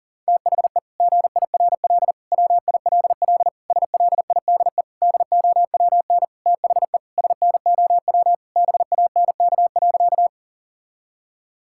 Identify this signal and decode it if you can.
Morse code, heard as a keyed tone, about 40 wpm, 700 Hz. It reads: THE GIRL WILL SLIDE DOWN THE SNOW BANK.
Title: Sentences from Top 1600 Words 40wpm